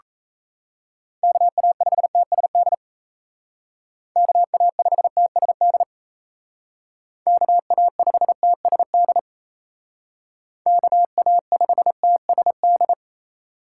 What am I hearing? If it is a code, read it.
Morse code, heard as a keyed tone, about 35 wpm, 700 Hz. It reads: KA5TSD KA5TSD KA5TSD KA5TSD